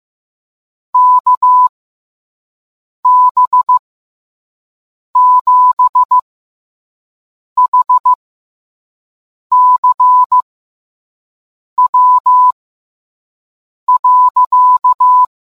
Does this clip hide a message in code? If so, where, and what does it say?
Yes: Morse code from 0.9 to 15.2 s, heard as a keyed tone, about 15 wpm, spaced as for 5 wpm, 1000 Hz. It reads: KB7HCW.